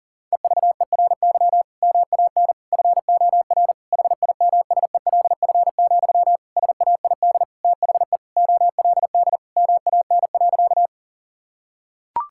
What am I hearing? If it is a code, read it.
Morse code, heard as a keyed tone, about 40 wpm, 700 Hz. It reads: EVERY MAN FOR HIMSELF, SAID THE OLD MAN.
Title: Sentences from Top 1600 Words 40wpm